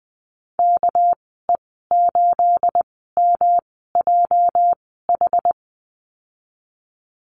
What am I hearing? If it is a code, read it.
Morse code, heard as a keyed tone, about 20 wpm, spaced as for 15 wpm, 700 Hz. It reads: KE8MJH
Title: 219 Lesson 35g - Number 0 - Call Signs 20wpm